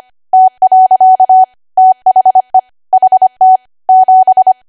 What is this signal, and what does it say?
Morse code, heard as a keyed tone, about 25 wpm, 750 Hz. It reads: T. THE HT 7